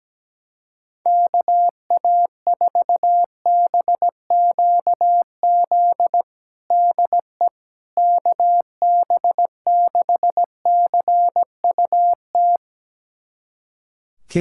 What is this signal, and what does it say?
Morse code, heard as a keyed tone, about 17 wpm, 700 Hz. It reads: KA4BQZ DE KB6CUT
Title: USA Callsign DE Callsign - 6 Characters - 17wpm